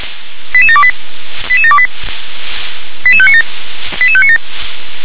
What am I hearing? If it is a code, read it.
Five-tone selective call (ZVEI), heard as quick runs of five tones, tones of about 70 ms. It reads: 88728 88728 88597 88597
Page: ZVEI